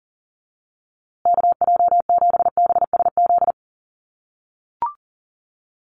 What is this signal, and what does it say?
Morse code, heard as a keyed tone, about 40 wpm, 700 Hz. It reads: KJ7BSZ
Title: USA Call Signs Repeated 2x 40wpm